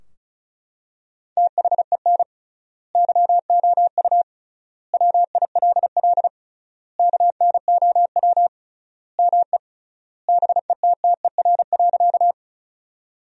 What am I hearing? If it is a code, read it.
Morse code, heard as a keyed tone, about 36 wpm, 700 Hz. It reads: THEN YOU WILL KNOW ME BETTER.
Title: 3x Word-Spacing - Sentences from Top 400 Words - 35wpm